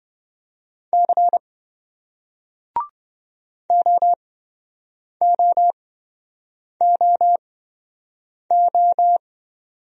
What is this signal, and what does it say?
Morse code, heard as a keyed tone, about 27 wpm, spaced as for 7 wpm, 700 Hz. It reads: C OOOO